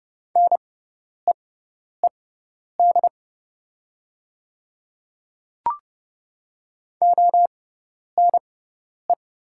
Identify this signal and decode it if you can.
Morse code, heard as a keyed tone, about 30 wpm, spaced as for 10 wpm, 700 Hz. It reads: NEED ONE
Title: 6x Character Spacing - Top 200 Words - 30wpm